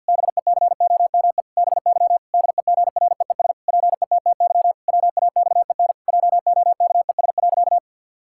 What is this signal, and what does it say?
Morse code, heard as a keyed tone, about 46 wpm, 700 Hz. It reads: BECOME BY DEGREES PETTY WAXEN JOKES.
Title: Pangram Challenge v2 - 50wpm